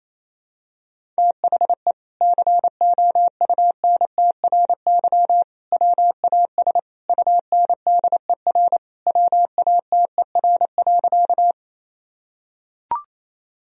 Morse code, heard as a keyed tone, about 28 wpm, 700 Hz. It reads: THE COUNTRY WAS UNDER WATER.
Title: Sentences from Top 1000 Words - Review Part 1 - 28wpm